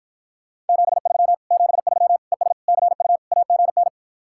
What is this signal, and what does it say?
Morse code, heard as a keyed tone, about 52 wpm, 700 Hz. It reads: 73 73 ES CU AGN